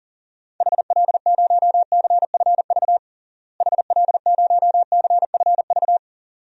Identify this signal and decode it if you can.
Morse code, heard as a keyed tone, about 37 wpm, 700 Hz. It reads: HL0CFV HL0CFV
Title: DXCC-100 Call Signs - 6 Characters - Repeated 2x - 40wpm